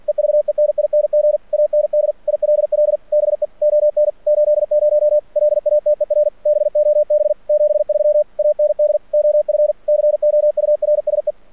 Morse code, heard as a keyed tone, about 46 wpm, 600 Hz. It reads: EVENING AND ILL BE ON 80 LATER BOB 73 AND OF COURSE
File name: archived / CRSnet